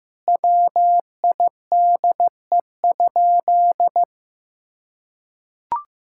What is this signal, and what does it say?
Morse code, heard as a keyed tone, about 15 wpm, 700 Hz. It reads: WIDE?
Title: Sentences from Top 1000 Words - Review Part 6 - 15wpm